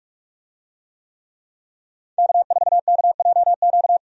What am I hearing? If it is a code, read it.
Morse code, heard as a keyed tone, about 46 wpm, 700 Hz. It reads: K4KJQ
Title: USA Call Signs - 5 Characters - 45wpm